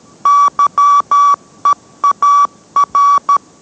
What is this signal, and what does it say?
Morse code, heard as a keyed tone, about 13 wpm, 1140 Hz. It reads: YEAR